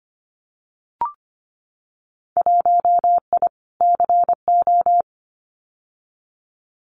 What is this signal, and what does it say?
Morse code, heard as a keyed tone, about 25 wpm, 700 Hz. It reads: 1I CO
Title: Winter Field Day Exchanges Encore 25wpm